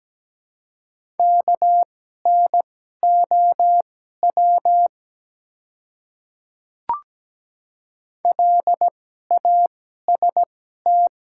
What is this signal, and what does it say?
Morse code, heard as a keyed tone, about 17 wpm, spaced as for 12 wpm, 700 Hz. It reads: KNOW LAST
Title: 2x Character Spacing - Top 300 Words - 17wpm